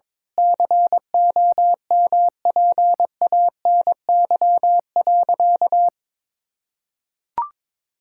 Morse code, heard as a keyed tone, about 22 wpm, 700 Hz. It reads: COMPANY.
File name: practice / 22wpm / Sentences with 8-Word Phrases - 22wpm